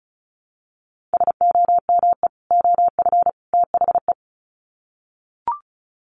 Morse code, heard as a keyed tone, about 34 wpm, 700 Hz. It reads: SOME OF THE
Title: Sets of 3 Words 35wpm